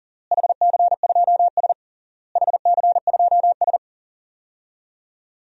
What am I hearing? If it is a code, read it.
Morse code, heard as a keyed tone, about 37 wpm, 700 Hz. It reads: HC2S HC2S
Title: DXCC-100 Call Signs - 4 Characters - Repeated 3x - 40wpm